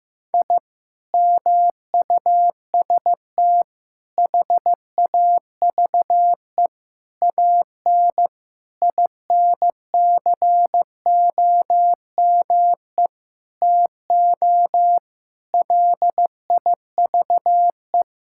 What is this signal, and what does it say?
Morse code, heard as a keyed tone, about 15 wpm, 700 Hz. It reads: I MUST HAVE AN INCOME TO LIVE
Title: Sentences from Top 1000 Words - Review Part 5 - 15wpm